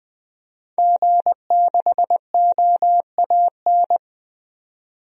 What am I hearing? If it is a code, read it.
Morse code, heard as a keyed tone, about 20 wpm, 700 Hz. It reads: G6OAN
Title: DXCC-100 Call Signs - 5 Characters - 20wpm